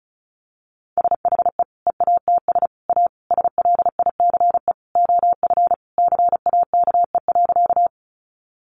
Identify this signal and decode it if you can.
Morse code, heard as a keyed tone, about 34 wpm, 700 Hz. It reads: SHE EATS A SLICE OF CAKE.
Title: Additional Easy Phrases 35wpm